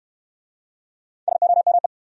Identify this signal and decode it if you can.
Morse code, heard as a keyed tone, about 69 wpm, 700 Hz. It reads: SOME